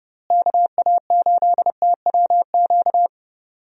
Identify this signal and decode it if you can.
Morse code, heard as a keyed tone, about 30 wpm, 700 Hz. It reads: KA8TWQ
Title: USA Call Signs - 6 Characters - Repeated 3x - 30wpm